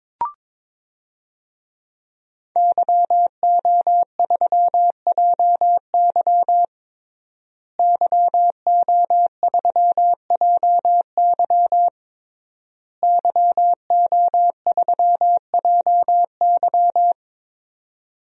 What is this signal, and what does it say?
Morse code, heard as a keyed tone, about 22 wpm, 700 Hz. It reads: YO3JY YO3JY YO3JY
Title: DXCC-100 Call Signs - 5 Characters - Repeated 3x - 22wpm